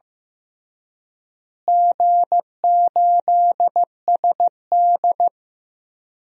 Morse code, heard as a keyed tone, about 15 wpm, 700 Hz. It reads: G8SD
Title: DXCC-100 Call Signs - 4 Characters - Repeated 2x - 15wpm